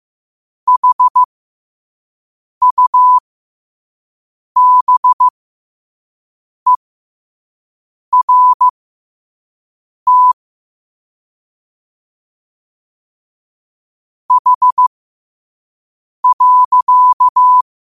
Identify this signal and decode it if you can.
Morse code, heard as a keyed tone, about 15 wpm, spaced as for 5 wpm, 1000 Hz. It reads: HUBERT H.
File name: qotdfiles / 5wpm